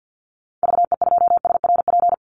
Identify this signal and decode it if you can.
Morse code, heard as a keyed tone, about 50 wpm, 700 Hz. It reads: VE2SRP
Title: Call Signs - 50wpm